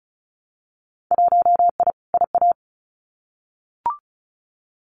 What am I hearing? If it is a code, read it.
Morse code, heard as a keyed tone, about 34 wpm, 700 Hz. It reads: 1I IA
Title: Winter Field Day Exchanges 35wpm